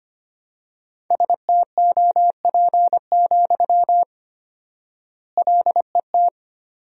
Morse code, heard as a keyed tone, about 25 wpm, 700 Hz. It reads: STOP, LET
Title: 4x Word-Spacing - Sentences from Top 300 Words - 25wpm